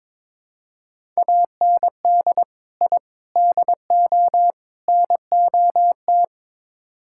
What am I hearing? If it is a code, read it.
Morse code, heard as a keyed tone, about 22 wpm, 700 Hz. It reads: AND I DO NOT
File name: practice / 22wpm / Build A Phrase - 5-Word Phrases - 22wpm